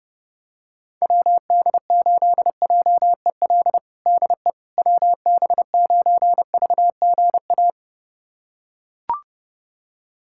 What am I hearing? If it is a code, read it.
Morse code, heard as a keyed tone, about 30 wpm, 700 Hz. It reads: WD8JEL DE WB9VGA
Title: USA Callsign DE Callsign - 6 Characters - 30wpm